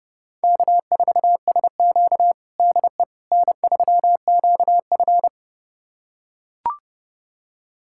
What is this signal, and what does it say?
Morse code, heard as a keyed tone, about 32 wpm, 700 Hz. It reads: K4SQ DE N3QF
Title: USA Callsign DE Callsign - 4 Characters - 30wpm